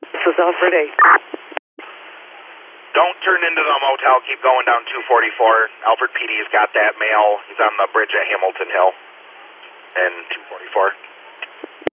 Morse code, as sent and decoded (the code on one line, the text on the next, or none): none